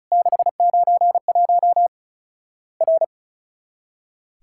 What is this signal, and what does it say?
Morse code, heard as a keyed tone, about 36 wpm, 700 Hz. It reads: 691 R
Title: Worked All Europe DX Contest - QTC Exchange - 35wpm